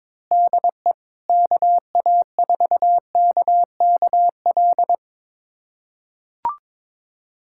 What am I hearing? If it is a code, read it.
Morse code, heard as a keyed tone, about 21 wpm, 700 Hz. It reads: DE KA4KKL
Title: USA Callsign DE Callsign - 6 Characters - 22wpm